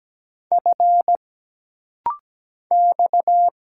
Morse code, heard as a keyed tone, about 17 wpm, 700 Hz. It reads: F X
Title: Single Letter-Number Alt - Rapid-Fire 17wpm